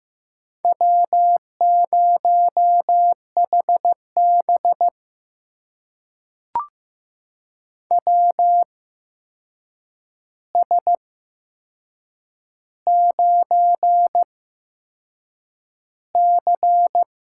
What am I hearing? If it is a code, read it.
Morse code, heard as a keyed tone, about 15 wpm, spaced as for 6 wpm, 700 Hz. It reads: W0HB WS9C
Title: USA Call Signs - 8x Character Spacing - 15wpm